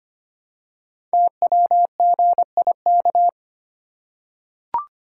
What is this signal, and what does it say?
Morse code, heard as a keyed tone, about 25 wpm, 700 Hz. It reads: TWGIK
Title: Code Groups - 5 Letters - 25wpm